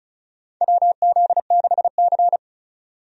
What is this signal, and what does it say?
Morse code, heard as a keyed tone, about 34 wpm, 700 Hz. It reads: WZ6C
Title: USA Call Signs - 4 Characters - 35wpm